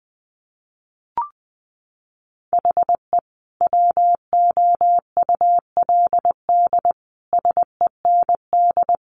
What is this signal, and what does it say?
Morse code, heard as a keyed tone, about 20 wpm, 700 Hz. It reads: HE WOULD SEND
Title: Sentences from Top 1500 Words 20wpm